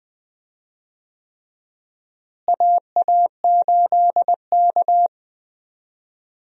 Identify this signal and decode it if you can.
Morse code, heard as a keyed tone, about 20 wpm, 700 Hz. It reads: AA8K
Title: USA Call Signs - 4 Characters - Repeated 3x - 20wpm